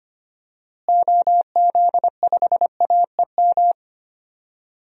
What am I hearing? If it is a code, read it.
Morse code, heard as a keyed tone, about 25 wpm, 700 Hz. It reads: OZ5AEM
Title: DXCC-100 Call Signs - 6 Characters - Repeated 3x - 25wpm